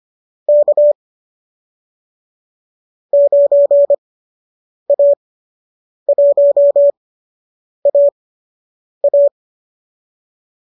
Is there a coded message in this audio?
Yes: Morse code from 0.5 to 9.3 s, heard as a keyed tone, about 25 wpm, spaced as for 8 wpm, 590 Hz. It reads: K 9A1AA